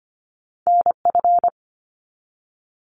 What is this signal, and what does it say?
Morse code, heard as a keyed tone, about 25 wpm, 700 Hz. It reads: NF
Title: Top Two Letter Combinations in Words 25wpm